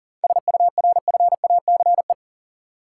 Morse code, heard as a keyed tone, about 37 wpm, 700 Hz. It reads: SURFACE
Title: Sentences from Top 600 Words - New Word Drill - 40wpm